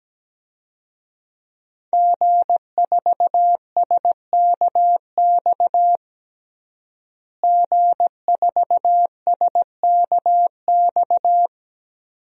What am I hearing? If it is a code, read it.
Morse code, heard as a keyed tone, about 17 wpm, 700 Hz. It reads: G4SKX G4SKX
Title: DXCC-100 Call Signs - 5 Characters - Repeated 2x - 17wpm